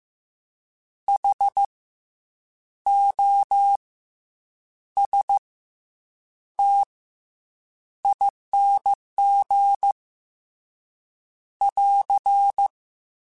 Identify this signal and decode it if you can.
Morse code, heard as a keyed tone, about 15 wpm, 800 Hz. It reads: H O S T ING <AR>
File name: Special INGs / Special INGs-mp3s / SlowToFaster ing @13